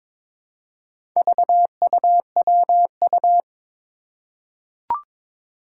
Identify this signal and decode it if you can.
Morse code, heard as a keyed tone, about 23 wpm, 700 Hz. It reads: VUWU
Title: Four Challenging Letter-Number Combinations 22wpm